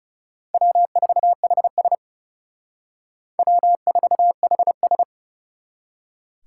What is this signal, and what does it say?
Morse code, heard as a keyed tone, about 38 wpm, 700 Hz. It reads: W4HS W4HS
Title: Speed Racing - USA Call Signs 30wpm